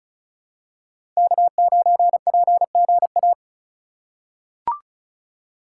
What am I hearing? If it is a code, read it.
Morse code, heard as a keyed tone, about 34 wpm, 700 Hz. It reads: K9PGA
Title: USA Call Signs - 5 Characters - 2x Character Spacing - 35wpm